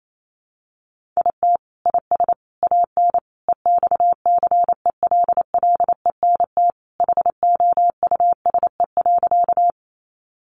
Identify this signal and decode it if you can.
Morse code, heard as a keyed tone, about 28 wpm, 700 Hz. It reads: IT IS AN EXCELLENT HOUSE.
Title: Sentences from Top 1000 Words 28wpm